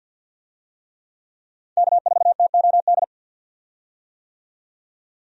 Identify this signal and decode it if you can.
Morse code, heard as a keyed tone, about 52 wpm, 700 Hz. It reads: K4TOD
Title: USA Call Signs - 5 Characters - 50wpm